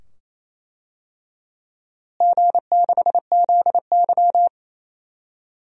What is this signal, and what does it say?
Morse code, heard as a keyed tone, about 27 wpm, 700 Hz. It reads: G6ZY
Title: DXCC-100 Call Signs - 4 Characters - Repeated 2x - 28wpm